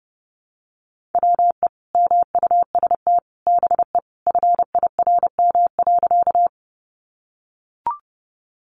Morse code, heard as a keyed tone, about 30 wpm, 700 Hz. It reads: WE MUST BE FIRM.
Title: Sentences from Top 1400 Words 30wpm